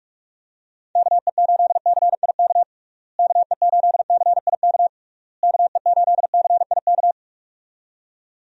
Morse code, heard as a keyed tone, about 42 wpm, 700 Hz. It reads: KE8CIK KE8CIK KE8CIK
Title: USA Call Signs - 6 Characters - Repeated 3x - 45wpm